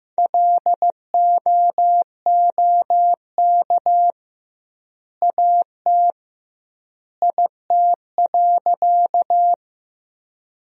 Morse code, heard as a keyed tone, about 15 wpm, 700 Hz. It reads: LOOK AT IT.
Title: 2x Word-Spacing - Sentences from Top 100 Words - 15wpm